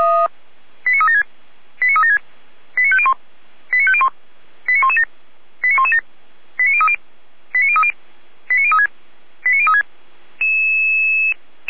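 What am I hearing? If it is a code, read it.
Five-tone selective call (ZVEI), heard as quick runs of five tones, tones of about 70 ms. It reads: 89376 89376 89501 89501 89117 89117 89030 89030 89036 89036
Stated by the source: ZVEI five-tone call received by SDR.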